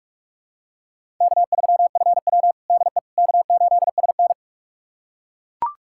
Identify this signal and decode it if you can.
Morse code, heard as a keyed tone, about 41 wpm, 700 Hz. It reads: K3FW DE K8SN
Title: USA Callsign DE Callsign - 4 Characters - 45wpm